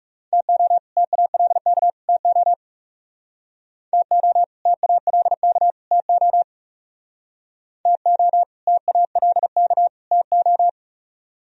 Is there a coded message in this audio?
Yes: Morse code from 0.3 to 10.7 s, heard as a keyed tone, about 39 wpm, 700 Hz. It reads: TO TALK TO TO TALK TO TO TALK TO